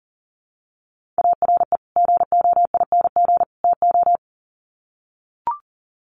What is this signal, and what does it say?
Morse code, heard as a keyed tone, about 40 wpm, 700 Hz. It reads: ARE GOING TO
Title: Sets of 3 Words 40wpm